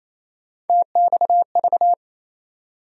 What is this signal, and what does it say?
Morse code, heard as a keyed tone, about 28 wpm, 700 Hz. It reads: TXV
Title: USA Call Sign Postfixes - 3 Letters 28wpm